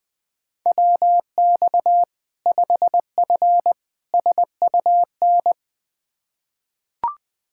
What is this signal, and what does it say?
Morse code, heard as a keyed tone, about 21 wpm, 700 Hz. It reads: WX 5F SUN
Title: Practice QSO - Exchange 2 - Info - 20wpm